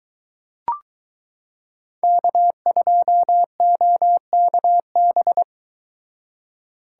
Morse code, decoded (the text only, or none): K2OKB